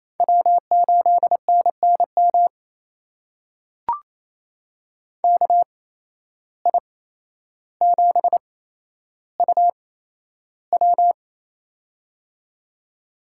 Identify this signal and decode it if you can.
Morse code, heard as a keyed tone, about 28 wpm, spaced as for 12 wpm, 700 Hz. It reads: W8NNM KI7UW
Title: USA Call Signs - 5 Characters - 8x Character Spacing - 28wpm